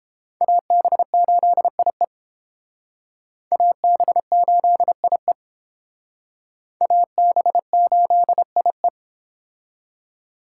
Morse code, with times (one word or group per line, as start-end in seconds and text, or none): 0.4-2.0 AB8IE
3.5-5.3 AB8IE
6.8-8.9 AB8IE